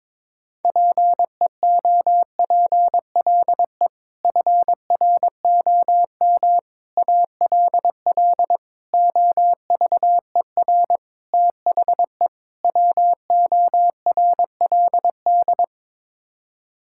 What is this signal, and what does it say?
Morse code, heard as a keyed tone, about 22 wpm, 700 Hz. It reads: PEOPLE FROM ALL OVER THE WORLD
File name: practice / 22wpm / Sets of 6 Words - 22wpm